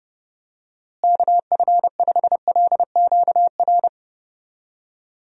Morse code, heard as a keyed tone, about 30 wpm, 700 Hz. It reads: KF5LQR
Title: USA Call Signs - 6 Characters - Repeated 3x - 30wpm